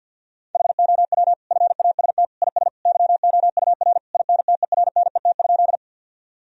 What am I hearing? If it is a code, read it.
Morse code, heard as a keyed tone, about 50 wpm, 700 Hz. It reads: HOW FAST IS YOUR INTERNET?
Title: Sentences from Top 1000 Words - Review Part 4 - 50wpm